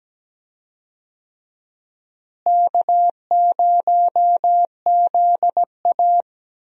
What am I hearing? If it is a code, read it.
Morse code, heard as a keyed tone, about 17 wpm, 700 Hz. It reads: K0ZA
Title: USA Call Signs - 4 Characters - Repeated 2x - 17wpm